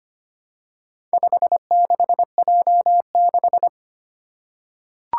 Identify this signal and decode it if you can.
Morse code, heard as a keyed tone, about 25 wpm, 700 Hz. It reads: 56J6
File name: practice / 25wpm / Four Letter-Number Combinations - 25wpm